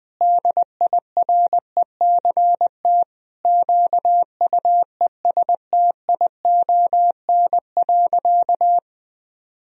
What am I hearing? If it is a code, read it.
Morse code, heard as a keyed tone, about 20 wpm, 700 Hz. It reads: DIRECT QUESTION.
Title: Sentences from Top 1000 Words - Review Part 2 - 20wpm